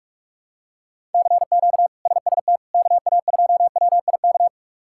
Morse code, heard as a keyed tone, about 46 wpm, 700 Hz. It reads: CQ SST KA2WIK
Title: K1USN SST QSO Practice - 45wpm